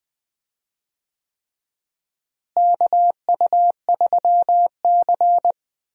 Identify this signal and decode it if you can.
Morse code, heard as a keyed tone, about 20 wpm, 700 Hz. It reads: KU3C
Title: USA Call Signs - 4 Characters - Repeated 3x - 20wpm